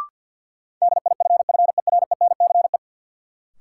Morse code, heard as a keyed tone, about 52 wpm, 700 Hz. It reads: DIFFERENCE
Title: Sentences from Top 800 Words - New Word Drill - 50wpm